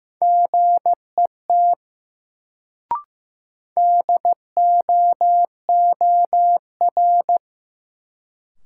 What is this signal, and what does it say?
Morse code, heard as a keyed tone, about 15 wpm, 700 Hz. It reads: GET DOOR
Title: Sets of 2 Words - New Word Drill - 15wpm